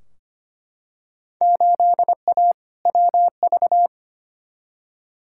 Morse code, heard as a keyed tone, about 25 wpm, 700 Hz. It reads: 8A WV
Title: Field Day Exchanges Encore - 25wpm